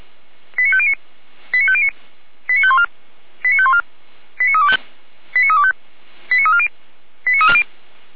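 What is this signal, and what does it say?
Five-tone selective call (ZVEI), heard as quick runs of five tones, tones of about 70 ms. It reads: 89509 89509 89514 89514 89325 89325 89340 89340
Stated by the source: ZVEI five-tone selective call received by SDR, from the audio archive of the Feuerwehr Orlamünde fire brigade.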